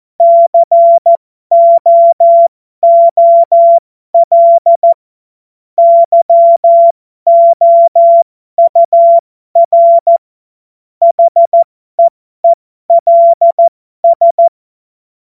Morse code, heard as a keyed tone, about 14 wpm, spaced as for 12 wpm, 680 Hz. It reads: COOL YOUR HEELS
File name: Morsekurs / Witze